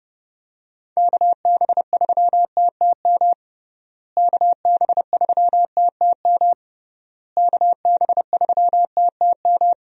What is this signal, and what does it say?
Morse code, heard as a keyed tone, about 30 wpm, 700 Hz. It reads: KB3TTM KB3TTM KB3TTM
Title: USA Call Signs - 6 Characters - Repeated 3x - 30wpm